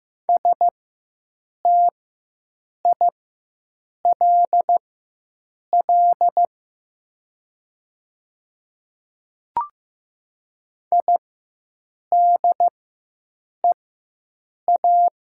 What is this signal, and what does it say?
Morse code, heard as a keyed tone, about 15 wpm, spaced as for 7 wpm, 700 Hz. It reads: STILL IDEA
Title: 4x Character Spacing - Top 300 Words - 15wpm